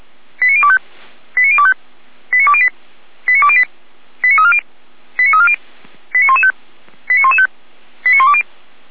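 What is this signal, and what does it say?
Five-tone selective call (ZVEI), heard as quick runs of five tones, tones of about 70 ms. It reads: 89025 89025 89208 89208 89340 89340 89115 89115 89120
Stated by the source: Five-tone ZVEI selective call received on SDR, from the Feuerwehr Orlamünde audio archive.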